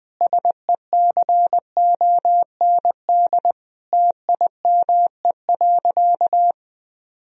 Morse code, heard as a keyed tone, about 20 wpm, 700 Hz. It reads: SECOND TIME.
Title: Sentences from Top 1000 Words - Review Part 1 - 20wpm